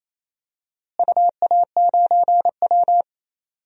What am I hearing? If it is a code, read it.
Morse code, heard as a keyed tone, about 28 wpm, 700 Hz. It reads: UA9W
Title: DXCC-100 Call Signs - 4 Characters - Repeated 2x - 28wpm